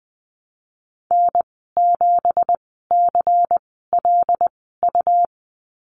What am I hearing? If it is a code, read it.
Morse code, heard as a keyed tone, about 20 wpm, spaced as for 15 wpm, 700 Hz. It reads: N7CLU
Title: 173 Lesson 28g - Number 9 - Call Signs 20wpm